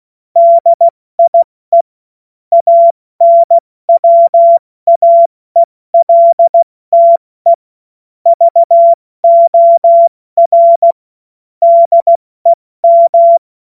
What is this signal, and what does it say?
Morse code, heard as a keyed tone, about 16 wpm, spaced as for 14 wpm, 680 Hz. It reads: DIE ANWAELTE VOR DEM
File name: Morsekurs / Witze / Witz 31. Mai 2022 70 F80